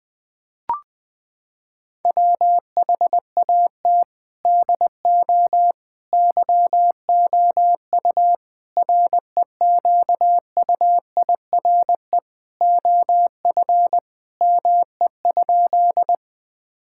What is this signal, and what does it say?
Morse code, heard as a keyed tone, about 20 wpm, 700 Hz. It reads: WHAT DO YOU REQUIRE OF ME?
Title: Sentences from Top 1000 Words - Review Part 6 - 20wpm